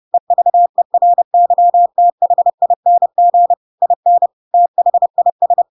Morse code, heard as a keyed tone, about 30 wpm, 700 Hz. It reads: EVERYTHING IN THIS
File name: Pinocchio / Pinocchio ch7 30 30 ik1hge